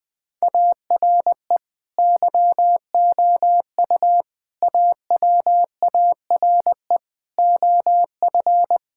Morse code, heard as a keyed tone, about 21 wpm, 700 Hz. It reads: ARE YOU AWARE OF
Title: Sentences with 2-Word Phrases - 20wpm